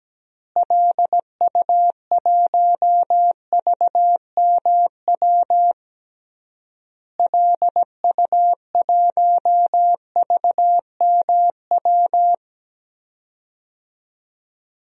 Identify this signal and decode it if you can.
Morse code, heard as a keyed tone, about 17 wpm, 700 Hz. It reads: LU1VMW LU1VMW
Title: DXCC-100 Call Signs - 6 Characters - Repeated 3x - 17wpm